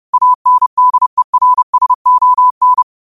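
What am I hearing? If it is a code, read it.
Morse code, heard as a keyed tone, about 30 wpm, 1000 Hz. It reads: ANDERSON
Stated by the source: Morse code quotes of the day (QOTD) sent at 30 words per minute.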